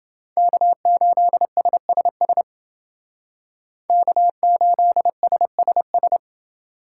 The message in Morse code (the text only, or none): K8SSS K8SSS